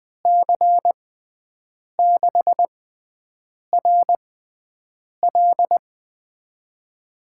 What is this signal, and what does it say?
Morse code, heard as a keyed tone, about 21 wpm, spaced as for 7 wpm, 700 Hz. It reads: C6RL
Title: USA Call Signs - 5 Characters - 6x Character Spacing - 20wpm